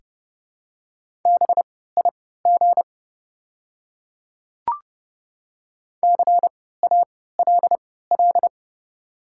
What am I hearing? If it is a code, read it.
Morse code, heard as a keyed tone, about 30 wpm, spaced as for 17 wpm, 700 Hz. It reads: BIG CALL
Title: 3x Character Spacing - Top 300 Words - 30wpm